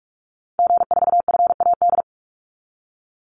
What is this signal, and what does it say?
Morse code, heard as a keyed tone, about 46 wpm, 700 Hz. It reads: G4FAD
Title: Call Signs - 45wpm